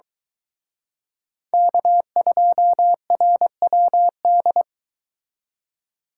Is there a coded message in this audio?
Yes: Morse code from 1.5 to 4.6 s, heard as a keyed tone, about 23 wpm, 700 Hz. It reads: K2RWD